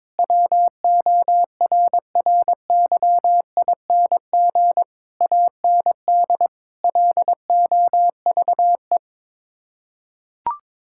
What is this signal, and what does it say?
Morse code, heard as a keyed tone, about 22 wpm, 700 Hz. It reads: WORRYING AND LOVE
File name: practice / 22wpm / Sets of 8 Words - 22wpm